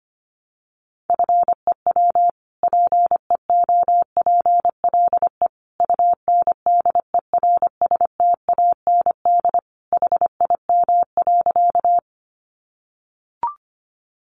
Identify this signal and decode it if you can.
Morse code, heard as a keyed tone, about 25 wpm, 700 Hz. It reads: FEW PEOPLE UNDERSTAND HIM.
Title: Sentences from Top 600 Words 25wpm